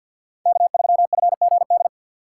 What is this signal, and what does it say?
Morse code, heard as a keyed tone, about 53 wpm, 700 Hz. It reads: K3FGD
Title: USA Call Signs - 5 Characters - 6x Character Spacing - 50wpm